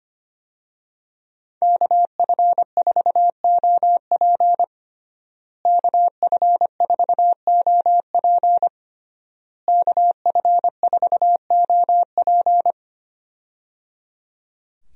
Morse code, heard as a keyed tone, about 25 wpm, 700 Hz. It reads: KF4OP KF4OP KF4OP
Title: USA Call Signs - 5 Characters - Repeated 3x - 25wpm